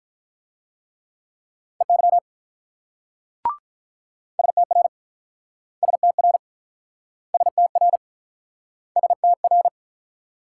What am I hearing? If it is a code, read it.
Morse code, heard as a keyed tone, about 45 wpm, 700 Hz. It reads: EY STR STR STR STR